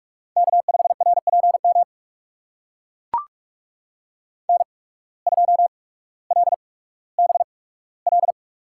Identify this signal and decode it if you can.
Morse code, heard as a keyed tone, about 44 wpm, spaced as for 19 wpm, 700 Hz. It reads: K5RPM N2LBL
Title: USA Call Signs - 5 Characters - 8x Character Spacing - 45wpm